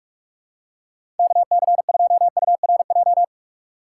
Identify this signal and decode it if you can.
Morse code, heard as a keyed tone, about 46 wpm, 700 Hz. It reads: KC2URJ
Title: USA Call Signs - 6 Characters - 6x Character Spacing - 45wpm